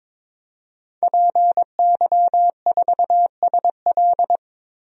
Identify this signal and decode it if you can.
Morse code, heard as a keyed tone, about 22 wpm, 700 Hz. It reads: PY4SL
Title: DXCC-100 Call Signs - 5 Characters - Repeated 3x - 22wpm